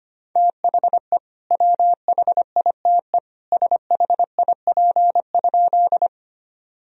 Morse code, heard as a keyed tone, about 25 wpm, 700 Hz. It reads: THE WHITE SHIP?
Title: Sentences from Top 1000 Words - Review Part 2 - 25wpm